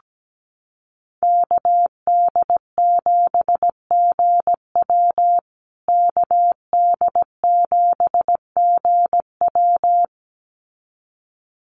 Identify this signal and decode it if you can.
Morse code, heard as a keyed tone, about 17 wpm, 700 Hz. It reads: KD7GW KD7GW
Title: USA Call Signs Repeated 2x 17wpm